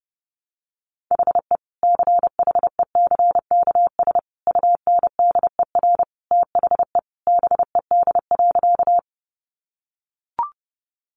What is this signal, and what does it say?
Morse code, heard as a keyed tone, about 30 wpm, 700 Hz. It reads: HE CHECKS UNDER THE BED.
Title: Easy Phrases 30wpm